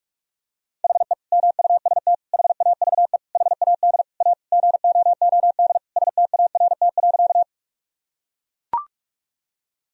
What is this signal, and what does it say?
Morse code, heard as a keyed tone, about 41 wpm, 700 Hz. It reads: HE MUST HAVE HAD A GOOD START.
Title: Sentences from Top 1000 Words - Review Part 1 - 45wpm